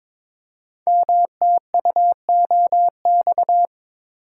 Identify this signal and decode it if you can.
Morse code, heard as a keyed tone, about 22 wpm, 700 Hz. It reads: MTUOX
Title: Code Groups - 5 Letters - 22wpm